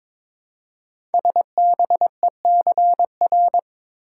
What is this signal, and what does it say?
Morse code, heard as a keyed tone, about 23 wpm, 700 Hz. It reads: SBECR
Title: Code Groups - 5 Letters - 22wpm